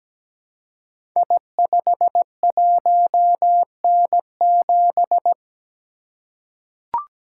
Morse code, heard as a keyed tone, about 17 wpm, 700 Hz. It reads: I51N7
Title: Code Groups - 5 Letter:Numbers - 17wpm